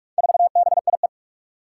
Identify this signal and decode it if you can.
Morse code, heard as a keyed tone, about 44 wpm, 700 Hz. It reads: 4BIE
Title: Four Challenging Letter-Number Combinations 45wpm